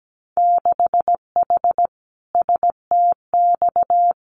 Morse code, heard as a keyed tone, about 17 wpm, 700 Hz. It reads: 6H STX
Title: Winter Field Day Exchanges Encore 17wpm